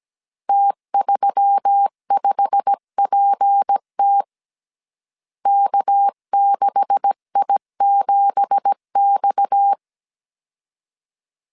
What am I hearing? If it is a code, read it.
Morse code, heard as a keyed tone, about 17 wpm, 800 Hz. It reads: T35PT K6I7X
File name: CW / DET / 1 9 1500C S B 16wpm 2856